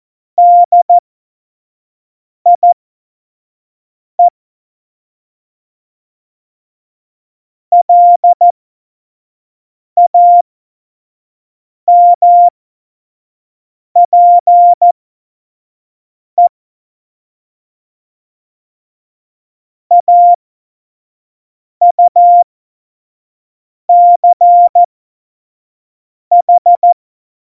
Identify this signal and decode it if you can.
Morse code, heard as a keyed tone, about 14 wpm, spaced as for 5 wpm, 700 Hz. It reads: DIE LAMPE AUCH
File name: Morsekurs / Witze / Witz 26. Februar 2019, 25 F70